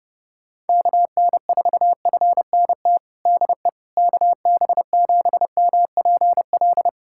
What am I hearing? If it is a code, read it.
Morse code, heard as a keyed tone, about 30 wpm, 700 Hz. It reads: KN4FNT DE KB7MPL
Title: USA Callsign DE Callsign - 6 Characters - 30wpm